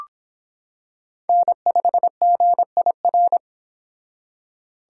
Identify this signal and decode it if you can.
Morse code, heard as a keyed tone, about 26 wpm, 700 Hz. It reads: N5GIR